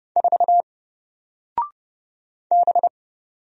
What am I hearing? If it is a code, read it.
Morse code, heard as a keyed tone, about 29 wpm, 700 Hz. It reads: 4 B
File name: Single-Letter-Number-30wpm